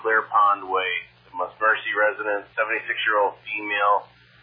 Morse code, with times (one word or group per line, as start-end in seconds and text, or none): none